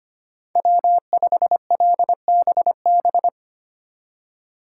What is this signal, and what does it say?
Morse code, heard as a keyed tone, about 25 wpm, 700 Hz. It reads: W5LBB
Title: USA Call Signs - 3x Character Spacing - 25wpm